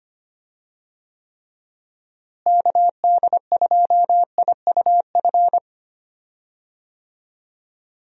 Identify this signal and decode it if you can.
Morse code, heard as a keyed tone, about 25 wpm, 700 Hz. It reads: KD2IUF
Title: USA Call Signs - 6 Characters - Repeated 2x - 25wpm